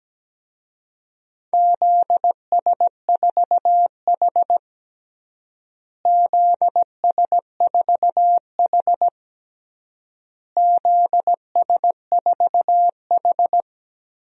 Morse code, heard as a keyed tone, about 17 wpm, 700 Hz. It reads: ZS4H ZS4H ZS4H
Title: DXCC-100 Call Signs - 4 Characters - Repeated 3x - 17wpm